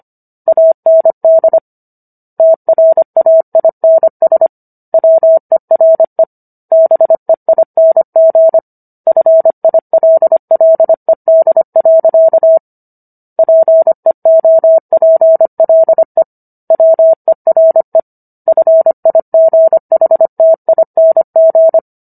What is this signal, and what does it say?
Morse code, heard as a keyed tone, about 25 wpm, 650 Hz. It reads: AND TRAINS WERE BEING FILLED. PEOPLE WERE FIGHTING